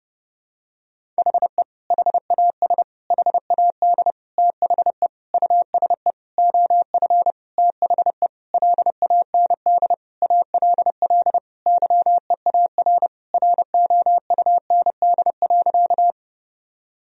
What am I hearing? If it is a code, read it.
Morse code, heard as a keyed tone, about 30 wpm, 700 Hz. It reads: HE HAS HAD THE USE OF THE LAND ALL YEAR ROUND.
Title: Sentences from Top 1000 Words - Review Part 1 - 30wpm